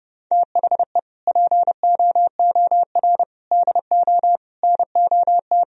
Morse code, heard as a keyed tone, about 30 wpm, 700 Hz. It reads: THE POOR DO NOT
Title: Sentences from Top 1000 Words - Review Part 2 - 30wpm